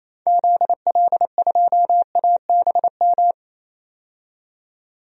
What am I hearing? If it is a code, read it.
Morse code, heard as a keyed tone, about 28 wpm, 700 Hz. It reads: ZL2ABM
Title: DXCC-100 Call Signs - 6 Characters - 28wpm